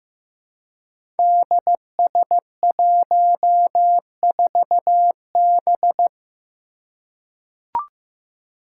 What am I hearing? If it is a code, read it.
Morse code, heard as a keyed tone, about 15 wpm, 700 Hz. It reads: DS14B
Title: Five Challenging Letter-Number Combinations 15wpm